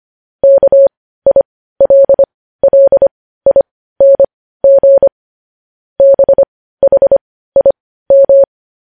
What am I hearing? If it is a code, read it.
Morse code, heard as a keyed tone, about 25 wpm, spaced as for 15 wpm, 550 Hz. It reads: KILLING BHIM
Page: news-headlines-in-morse-code-at-15-wpm